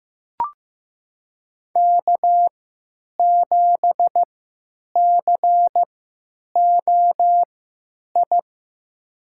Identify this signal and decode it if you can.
Morse code, heard as a keyed tone, about 15 wpm, spaced as for 9 wpm, 700 Hz. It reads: K7COI
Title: USA Call Signs - 5 Characters - 3x Character Spacing - 15wpm